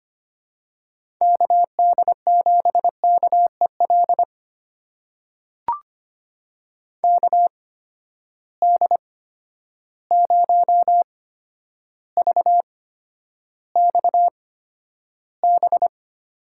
Morse code, heard as a keyed tone, about 25 wpm, spaced as for 11 wpm, 700 Hz. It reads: KD7KEL KD0VXB
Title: USA Call Signs - 6 Characters - 8x Character Spacing - 25wpm